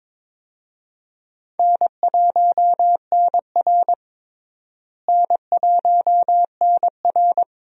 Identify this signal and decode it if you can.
Morse code, heard as a keyed tone, about 21 wpm, 700 Hz. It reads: N1NR N1NR
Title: USA Call Signs - 4 Characters - Repeated 2x - 22wpm